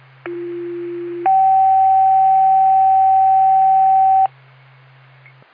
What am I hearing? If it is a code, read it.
Two-tone page, heard as a one-second tone, then a three-second tone, about 350 then 770 Hz. It is heard from 0.3 to 4.3 s.